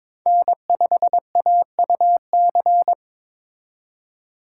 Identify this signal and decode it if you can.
Morse code, heard as a keyed tone, about 26 wpm, 700 Hz. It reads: N5AUC
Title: Speed Racing - USA Call Signs 22wpm